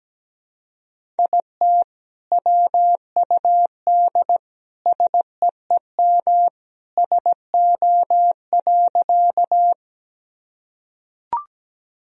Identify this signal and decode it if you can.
Morse code, heard as a keyed tone, about 17 wpm, 700 Hz. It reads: IT WUD SEEM SO.
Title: Sentences from Top 300 Words and Abbreviations 17wpm